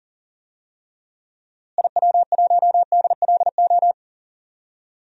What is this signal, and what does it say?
Morse code, heard as a keyed tone, about 40 wpm, 700 Hz. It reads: IW1DLO
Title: DXCC-100 Call Signs - 6 Characters - Repeated 2x - 40wpm